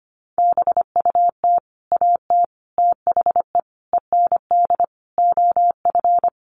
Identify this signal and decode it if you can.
Morse code, heard as a keyed tone, about 25 wpm, 700 Hz. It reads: BUT AT THE END OF
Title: Sets of 5 Words Encore2 25wpm